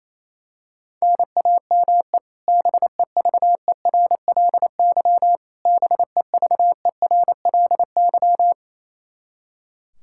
Morse code, heard as a keyed tone, about 28 wpm, 700 Hz. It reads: NAME BEVERLY BEVERLY
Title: Practice QSO - Exchange 1 - Beginning - 28wpm